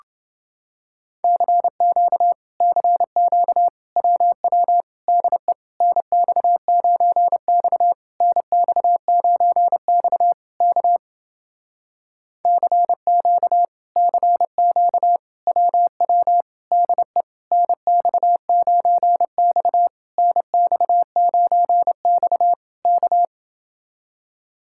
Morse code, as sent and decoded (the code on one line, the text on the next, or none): -.-. --.- / -.-. --.- / .-- .-- / -.. / -. -..- ----. -..- / -. -..- ----. -..- / -.- / -.-. --.- / -.-. --.- / .-- .-- / -.. / -. -..- ----. -..- / -. -..- ----. -..- / -.-
CQ CQ WW DE NX9X NX9X K CQ CQ WW DE NX9X NX9X K